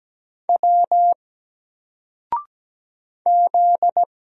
Morse code, heard as a keyed tone, about 17 wpm, 700 Hz. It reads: W Z
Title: Single Letter-Number Alt 17wpm